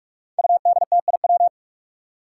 Morse code, heard as a keyed tone, about 46 wpm, 700 Hz. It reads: UDTIW
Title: Five Challenging Letter-Number Combinations 45wpm